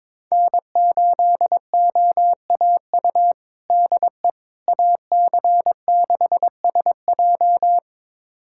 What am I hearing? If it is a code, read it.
Morse code, heard as a keyed tone, about 22 wpm, 700 Hz. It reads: N8OAU DE AC6SJ
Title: USA Callsign DE Callsign - 5 Characters - 22wpm